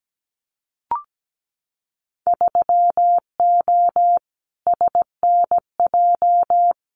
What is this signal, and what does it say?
Morse code, heard as a keyed tone, about 17 wpm, 700 Hz. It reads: 3O SNJ
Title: Winter Field Day Exchanges Encore 17wpm